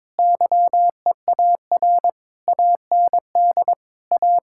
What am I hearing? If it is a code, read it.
Morse code, heard as a keyed tone, about 22 wpm, 700 Hz. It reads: YEAR AND A
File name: Sentences with 7-Word Phrases - 22wpm